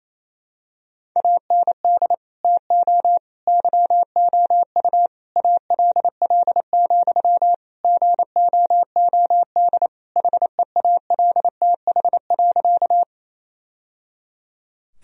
Morse code, heard as a keyed tone, about 27 wpm, 700 Hz. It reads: AND TO YOU ALL, GOOD HEALTH.
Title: Sentences from Top 1000 Words - Review Part 3 - 28wpm